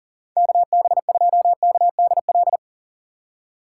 Morse code, heard as a keyed tone, about 37 wpm, 700 Hz. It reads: KB2KDL
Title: USA Call Signs - 6 Characters - 3x Character Spacing - 40wpm